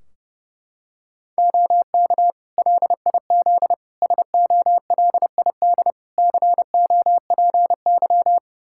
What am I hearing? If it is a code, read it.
Morse code, heard as a keyed tone, about 30 wpm, 700 Hz. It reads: OK LIZ SOLID COPY
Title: K1USN SST QSO Practice 30wpm